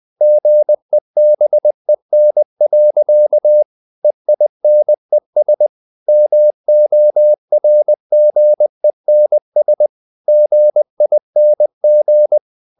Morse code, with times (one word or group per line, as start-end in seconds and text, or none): 0.2-3.6 GEBEN.
4.0-5.7 EINES
6.1-9.9 MORGENS
10.3-12.4 GING